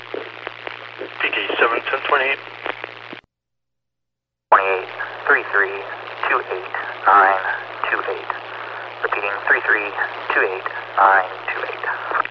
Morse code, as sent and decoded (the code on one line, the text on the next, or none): none